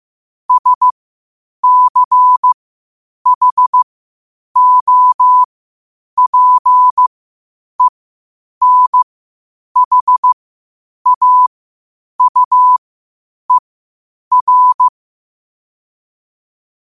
Morse code, heard as a keyed tone, about 15 wpm, spaced as for 8 wpm, 1000 Hz. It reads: SCHOPENHAUER
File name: qotdfiles / 8wpm